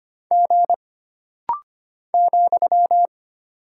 Morse code, heard as a keyed tone, about 25 wpm, 700 Hz. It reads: G ,
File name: Single Character (Letter:Number:Punctuation:Prosign) - Rapid-Fire - 25wpm